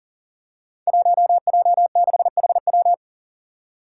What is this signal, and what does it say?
Morse code, heard as a keyed tone, about 40 wpm, 700 Hz. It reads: 1J6HW
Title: Five Challenging Letter-Number Combinations 40wpm